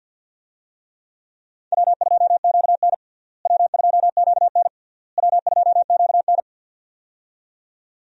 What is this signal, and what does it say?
Morse code, heard as a keyed tone, about 50 wpm, 700 Hz. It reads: W2QN W2QN W2QN
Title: USA Call Signs - 4 Characters - Repeated 3x - 50wpm